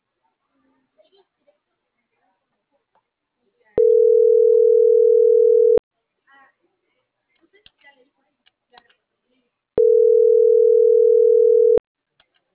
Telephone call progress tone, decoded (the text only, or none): ringback tone